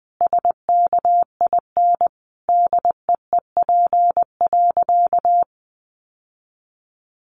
Morse code, heard as a keyed tone, about 20 wpm, 700 Hz. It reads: SKIN DEEP.
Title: Sentences from Top 1300 Words 20wpm